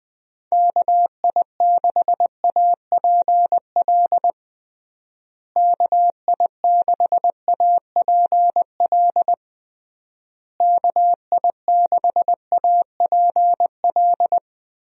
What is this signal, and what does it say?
Morse code, heard as a keyed tone, about 20 wpm, 700 Hz. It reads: KI6APL KI6APL KI6APL
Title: USA Call Signs - 6 Characters - Repeated 3x - 20wpm